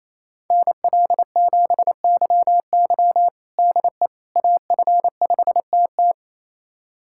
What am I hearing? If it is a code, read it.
Morse code, heard as a keyed tone, about 28 wpm, 700 Hz. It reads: NL7YY DE AF5TT
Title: USA Callsign DE Callsign - 5 Characters - 28wpm